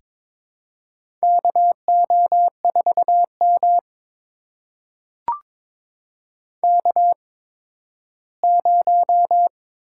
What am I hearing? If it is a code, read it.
Morse code, heard as a keyed tone, about 22 wpm, spaced as for 13 wpm, 700 Hz. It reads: KO4M K0
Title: USA Call Signs - 4 Characters - 8x Character Spacing - 22wpm